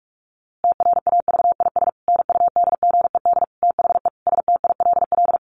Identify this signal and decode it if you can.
Morse code, heard as a keyed tone, about 44 wpm, 700 Hz. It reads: TRAVIS NUDGED THE STILL
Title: A Sound Of Thunder 45wpm